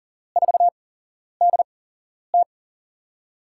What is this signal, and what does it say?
Morse code, heard as a keyed tone, about 39 wpm, spaced as for 11 wpm, 700 Hz. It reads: 4DT
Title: USA Call Signs - 5 Characters - 8x Character Spacing - 40wpm